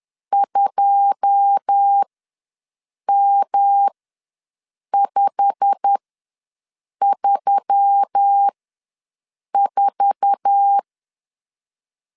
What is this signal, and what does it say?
Morse code, heard as a keyed tone, about 10 wpm, spaced as for 6 wpm, 800 Hz. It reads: 2M534
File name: LEZIONI / 9 2 ws 10wpm 4130